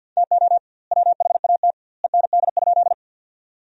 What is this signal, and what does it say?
Morse code, heard as a keyed tone, about 50 wpm, 700 Hz. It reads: TO WHAT END?
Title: 2x Word-Spacing - Sentences from Top 200 Words - 50wpm